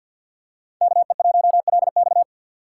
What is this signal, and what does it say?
Morse code, heard as a keyed tone, about 47 wpm, 700 Hz. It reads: KE1LX
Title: USA Call Signs - 5 Characters - 6x Character Spacing - 50wpm